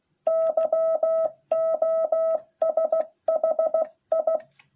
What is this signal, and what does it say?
Morse code, heard as a keyed tone, about 16 wpm, 660 Hz. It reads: YOSHI